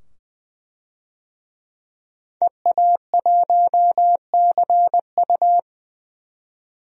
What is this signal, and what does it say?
Morse code, heard as a keyed tone, about 20 wpm, 700 Hz. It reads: EA1CU
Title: DXCC-100 Call Signs - 5 Characters - 20wpm